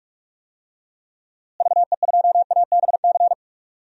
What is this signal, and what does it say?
Morse code, heard as a keyed tone, about 44 wpm, 700 Hz. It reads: VE2ABC
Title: DXCC-100 Call Signs - 6 Characters - Repeated 3x - 45wpm